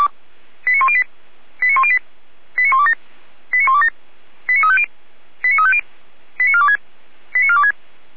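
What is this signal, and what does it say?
Five-tone selective call (ZVEI), heard as quick runs of five tones, tones of about 70 ms. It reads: 89108 89108 89127 89127 89350 89350 89426 89426
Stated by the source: Five-tone ZVEI selective call 89075, received by SDR.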